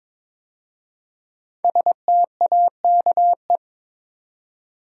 Morse code, heard as a keyed tone, about 22 wpm, 700 Hz. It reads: STAKE
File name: 5 Letter Words - First 3 Letters Given - 22wpm